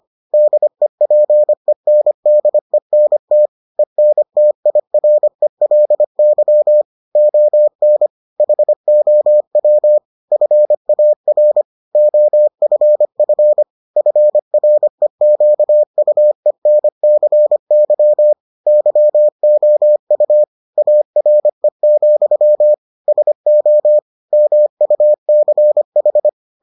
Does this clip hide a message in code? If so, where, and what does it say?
Yes: Morse code from 0.3 to 26.3 s, heard as a keyed tone, about 25 wpm, 600 Hz. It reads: DEPENDENT ENTIRELY ON HOW FAR OFF FREQUENCY YOU ARE, SO MUCH